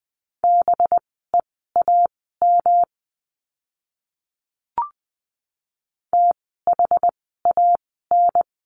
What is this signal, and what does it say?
Morse code, heard as a keyed tone, about 20 wpm, spaced as for 14 wpm, 700 Hz. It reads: BEAM THAN
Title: 185 Lesson 30f - Letter b - Words 20wpm